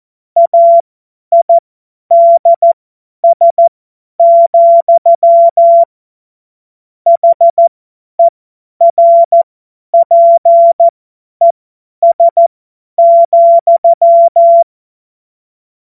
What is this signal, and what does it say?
Morse code, heard as a keyed tone, about 14 wpm, spaced as for 10 wpm, 680 Hz. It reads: AIDS, HERPES,
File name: Morsekurs / Witze / Witz 26 März 2024 50 F70